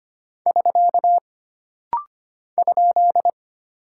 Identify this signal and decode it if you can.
Morse code, heard as a keyed tone, about 25 wpm, 700 Hz. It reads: <SK> ?
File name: Single Punctuation-Prosign - Rapid-Fire - 25wpm